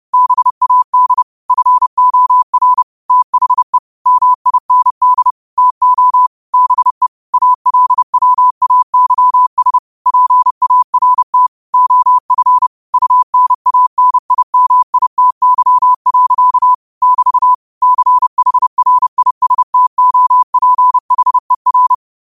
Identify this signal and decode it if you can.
Morse code, heard as a keyed tone, about 30 wpm, 1000 Hz. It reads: BAD FOR THE MIND TO BE ALWAYS PART OF UNANIMITY. <BT> CHRISTOPHER
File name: qotdfiles / 30wpm